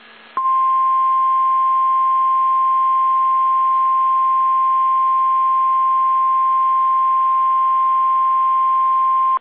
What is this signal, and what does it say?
Emergency Alert System (SAME) tones (the 1050 Hz attention tone). It is heard from 1.1 to 8.6 s.
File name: eas / KSDZKDJL / KSDZKDJL